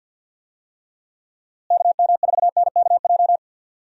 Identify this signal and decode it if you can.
Morse code, heard as a keyed tone, about 53 wpm, 700 Hz. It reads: KM4NKJ
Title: USA Call Signs - 6 Characters - Repeated 2x - 50wpm